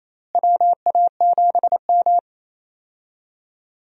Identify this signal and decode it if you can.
Morse code, heard as a keyed tone, about 28 wpm, 700 Hz. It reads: WA7M
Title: USA Call Signs - 4 Characters - Repeated 2x - 28wpm